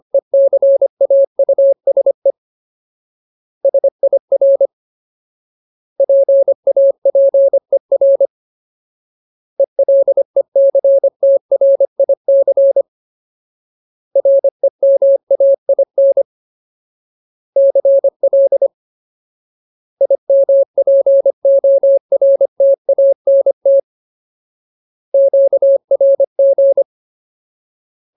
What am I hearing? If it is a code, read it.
Morse code, heard as a keyed tone, about 25 wpm, 550 Hz. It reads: ECAUSE SIR PAPER ELECTRIC REMAIN CL IMPORTANT QRG
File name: Ultimate Word List / 25WPM Ultimate